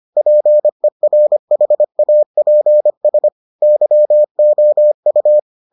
Morse code, heard as a keyed tone, about 25 wpm, 600 Hz. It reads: PERHAPS YOU